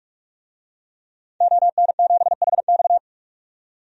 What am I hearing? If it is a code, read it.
Morse code, heard as a keyed tone, about 46 wpm, 700 Hz. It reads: ON7HX
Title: DXCC-100 Call Signs - 5 Characters - 45wpm